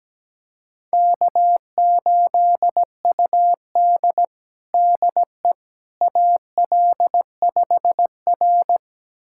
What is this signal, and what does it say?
Morse code, heard as a keyed tone, about 17 wpm, 700 Hz. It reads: K8UD DE AL5R
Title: USA Callsign DE Callsign - 4 Characters - 17wpm